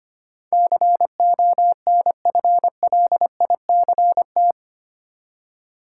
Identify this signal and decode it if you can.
Morse code, heard as a keyed tone, about 25 wpm, 690 Hz. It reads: CONFLICT
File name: 8 Letter Words - First 3 Letters Given - 25wpm